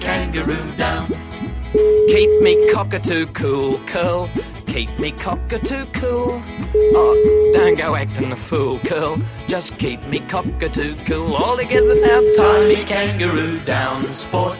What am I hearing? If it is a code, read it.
Ringback tone (Europe, 425 Hz), with speech.